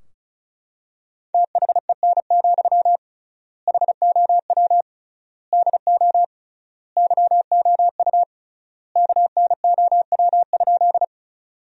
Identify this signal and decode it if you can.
Morse code, heard as a keyed tone, about 36 wpm, 700 Hz. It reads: THEN, HOW DO YOU KNOW?
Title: 3x Word-Spacing - Sentences from Top 100 Words - 35wpm